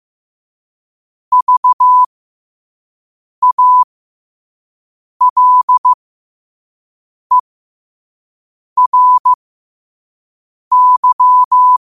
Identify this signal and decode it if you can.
Morse code, heard as a keyed tone, about 15 wpm, spaced as for 5 wpm, 1000 Hz. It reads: VALERY
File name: qotdfiles / 5wpm